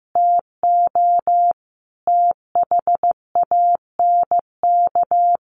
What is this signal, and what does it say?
Morse code, heard as a keyed tone, about 15 wpm, 700 Hz. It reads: TO THANK
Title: Sentences from Top 1000 Words 15wpm